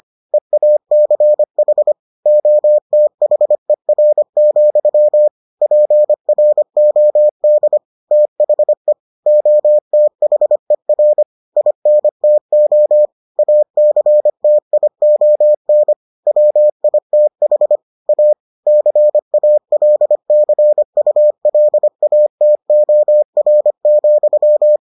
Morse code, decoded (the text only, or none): EACH OTHER, PROD THE OTHER INTO ACTION WITH A CALCULATOR,